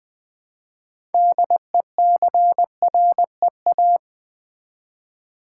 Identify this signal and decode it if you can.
Morse code, heard as a keyed tone, about 20 wpm, 700 Hz. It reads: DECREA
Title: Build A Word - 3 Letters At A Time - 8-Letter Words - 20wpm